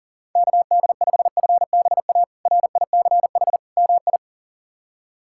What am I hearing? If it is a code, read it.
Morse code, heard as a keyed tone, about 40 wpm, 700 Hz. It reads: KD5FBA RICH MI
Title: K1USN SST QSO Practice - 40wpm